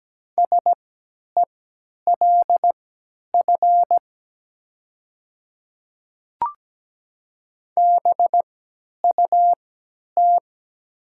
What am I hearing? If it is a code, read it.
Morse code, heard as a keyed tone, about 17 wpm, spaced as for 10 wpm, 700 Hz. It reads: SELF BUT
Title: 3x Character Spacing - Top 300 Words - 17wpm